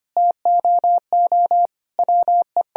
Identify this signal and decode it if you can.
Morse code, heard as a keyed tone, about 25 wpm, 700 Hz. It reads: TOO WE
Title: Sentences from Top 1000 Words - Review Part 3 - 25wpm